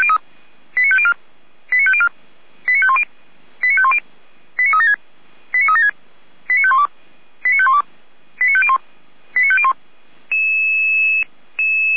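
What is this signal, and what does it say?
Five-tone selective call (ZVEI), heard as quick runs of five tones, tones of about 70 ms. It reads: 89504 89504 89510 89510 89376 89376 89512 89512 89501 89501
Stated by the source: ZVEI five-tone call 89502 received on SDR, fire brigade Orlamünde (feuerwehr-orlamuende).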